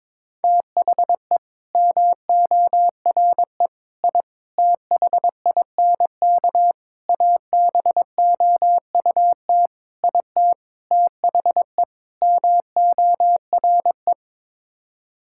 Morse code, heard as a keyed tone, about 22 wpm, 700 Hz. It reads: THE MORE I THINK ABOUT IT THE MORE
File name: practice / 22wpm / Sets of 8 Words - 22wpm